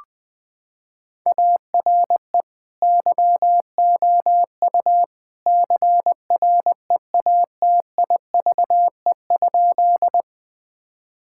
Morse code, heard as a keyed tone, about 21 wpm, 700 Hz. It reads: ARE YOU CREATIVE?
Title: Sentences from Top 1000 Words - Review Part 5 - 20wpm